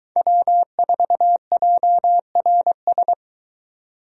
Morse code, heard as a keyed tone, about 23 wpm, 700 Hz. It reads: W4JRS